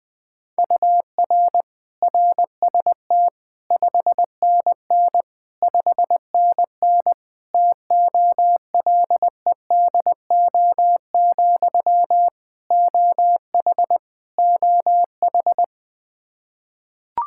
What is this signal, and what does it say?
Morse code, heard as a keyed tone, about 20 wpm, 700 Hz. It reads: UR RST 5NN 5NN TOLEDO, OH OH
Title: Practice QSO - Exchange 1 - Beginning - 20wpm